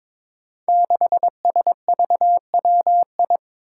Morse code, heard as a keyed tone, about 21 wpm, 700 Hz. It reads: 6SVWI
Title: Five Challenging Letter-Number Combinations 22wpm